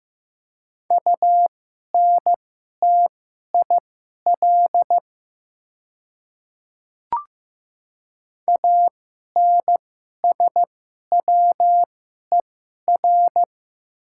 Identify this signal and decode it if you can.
Morse code, heard as a keyed tone, about 15 wpm, spaced as for 11 wpm, 700 Hz. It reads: UNTIL ANSWER
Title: 2x Character Spacing - Top 300 Words - 15wpm